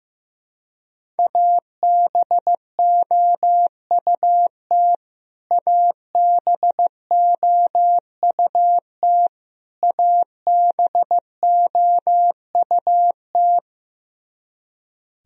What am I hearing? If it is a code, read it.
Morse code, heard as a keyed tone, about 15 wpm, 700 Hz. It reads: ABOUT ABOUT ABOUT
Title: Top 100 Words Repeated 15wpm